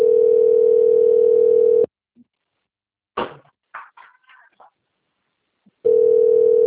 Ringback tone (North America), 440 + 480 Hz.